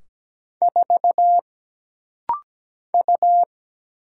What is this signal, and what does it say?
Morse code, heard as a keyed tone, about 17 wpm, 700 Hz. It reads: 4 U
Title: Single Letter-Number Alt - Rapid-Fire 17wpm